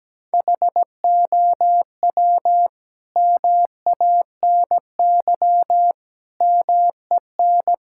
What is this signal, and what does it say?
Morse code, heard as a keyed tone, about 17 wpm, 700 Hz. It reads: HOW MANY MEN
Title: Sentences from Top 1600 Words 17wpm